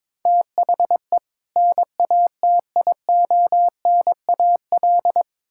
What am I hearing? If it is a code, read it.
Morse code, heard as a keyed tone, about 22 wpm, 700 Hz. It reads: THE NATIONAL
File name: Build A Phrase - 2-Word Phrases - 22wpm